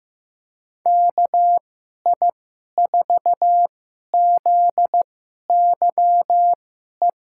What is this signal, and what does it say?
Morse code, heard as a keyed tone, about 15 wpm, spaced as for 11 wpm, 700 Hz. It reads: KI4ZYE
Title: USA Call Signs - 6 Characters - 2x Character Spacing - 15wpm